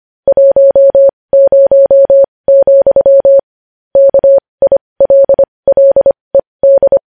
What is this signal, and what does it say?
Morse code, heard as a keyed tone, about 25 wpm, spaced as for 20 wpm, 550 Hz. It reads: 10, KILLED
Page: News Headlines in Morse Code at 20 WPM